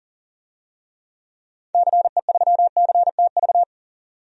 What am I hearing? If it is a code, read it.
Morse code, heard as a keyed tone, about 40 wpm, 700 Hz. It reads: CE3CTV
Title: DXCC-100 Call Signs - 6 Characters - 40wpm